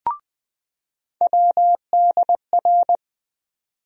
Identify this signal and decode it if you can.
Morse code, heard as a keyed tone, about 20 wpm, 700 Hz. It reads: WDR
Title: USA Call Sign Postfixes - 3 Letters 20wpm